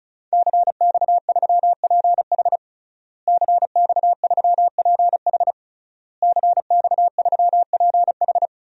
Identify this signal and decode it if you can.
Morse code, heard as a keyed tone, about 34 wpm, 700 Hz. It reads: CX3PH CX3PH CX3PH
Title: DXCC-100 Call Signs - 5 Characters - Repeated 3x - 35wpm